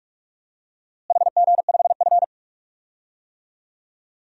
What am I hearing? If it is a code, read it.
Morse code, heard as a keyed tone, about 46 wpm, 700 Hz. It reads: HG5F
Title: DXCC-100 Call Signs - 4 Characters - Repeated 3x - 45wpm